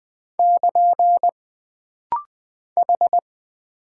Morse code, heard as a keyed tone, about 20 wpm, 700 Hz. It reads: <KN> H
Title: Single Character (Letter:Number:Punctuation:Prosign) - Rapid-Fire - 20wpm